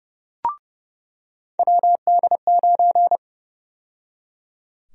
Morse code, heard as a keyed tone, about 30 wpm, 700 Hz. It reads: WD9
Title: USA Call Sign Prefixes - 30wpm